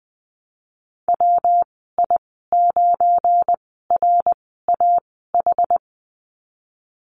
Morse code, heard as a keyed tone, about 20 wpm, spaced as for 15 wpm, 700 Hz. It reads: WI9RAH
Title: 173 Lesson 28g - Number 9 - Call Signs 20wpm